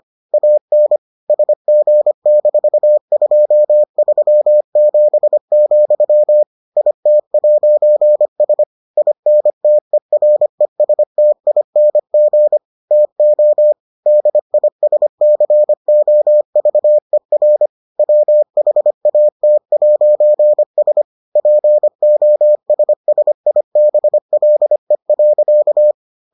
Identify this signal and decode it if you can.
Morse code, heard as a keyed tone, about 25 wpm, 600 Hz. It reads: AN SG-237, IT'S INTERESTING TO DISCOVER WHAT'S POSSIBLE.